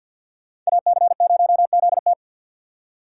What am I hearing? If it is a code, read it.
Morse code, heard as a keyed tone, about 62 wpm, 700 Hz. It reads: AC0ZT
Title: Speed Racing - USA 2x2 Call Signs 50wpm